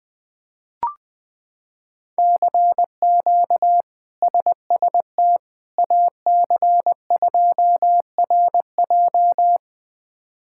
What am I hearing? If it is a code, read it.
Morse code, heard as a keyed tone, about 20 wpm, 700 Hz. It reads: CQ SST AC2RJ
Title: K1USN SST QSO Practice - 20wpm